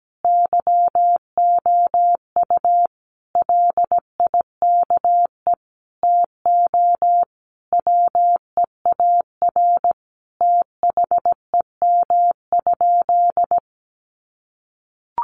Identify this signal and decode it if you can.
Morse code, heard as a keyed tone, about 17 wpm, 700 Hz. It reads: YOU LIKE TO WEAR THEM?
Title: Sentences from Top 1500 Words 17wpm